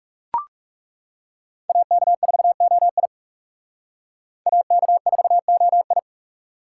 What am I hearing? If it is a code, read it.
Morse code, heard as a keyed tone, about 36 wpm, 700 Hz. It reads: AK4OI AK4OI
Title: Speed Racing - USA Call Signs 30wpm